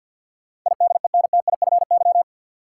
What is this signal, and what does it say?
Morse code, heard as a keyed tone, about 48 wpm, 700 Hz. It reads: IDENTIFY
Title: Sentences from Top 900 Words - New Word Drill - 50wpm